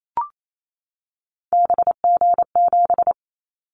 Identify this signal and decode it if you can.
Morse code, heard as a keyed tone, about 28 wpm, 700 Hz. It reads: BG7
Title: Callsign Prefixes for Top 100 DXCC - 28wpm